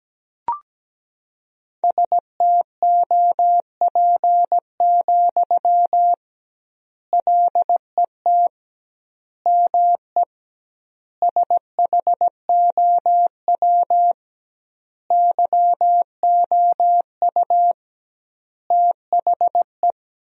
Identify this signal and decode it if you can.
Morse code, heard as a keyed tone, about 17 wpm, 700 Hz. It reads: STOP, LET ME SHOW YOU THE
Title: 2x Word-Spacing - Sentences from Top 300 Words - 17wpm